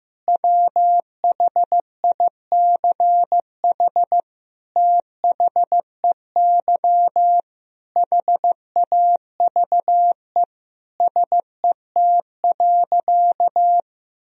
Morse code, heard as a keyed tone, about 15 wpm, 700 Hz. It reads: WHICH THEY HAVE SET.
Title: Sentences from Top 1000 Words - Review Part 2 - 15wpm